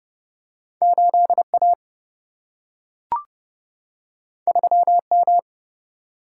Morse code, heard as a keyed tone, about 31 wpm, 700 Hz. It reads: 8A 3M
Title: Single-Letter Single-Number 30wpm